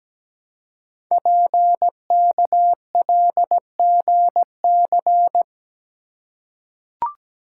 Morse code, heard as a keyed tone, about 17 wpm, 700 Hz. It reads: PKLGC
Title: Code Groups - 5 Letter:Numbers - 17wpm